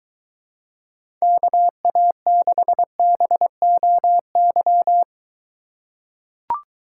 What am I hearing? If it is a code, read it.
Morse code, heard as a keyed tone, about 22 wpm, 700 Hz. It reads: KA6BOY